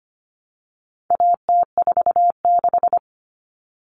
Morse code, heard as a keyed tone, about 25 wpm, 700 Hz. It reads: AT46
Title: All Cut Numbers - 4-Digit 25wpm